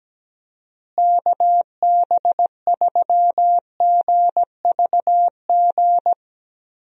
Morse code, heard as a keyed tone, about 17 wpm, 700 Hz. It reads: KB3GVG
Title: USA Call Signs - 6 Characters - Repeated 3x - 17wpm